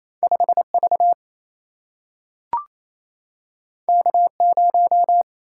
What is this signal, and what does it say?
Morse code, heard as a keyed tone, about 28 wpm, 700 Hz. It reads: 5 V K 0
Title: Single-Letter Single-Number 28wpm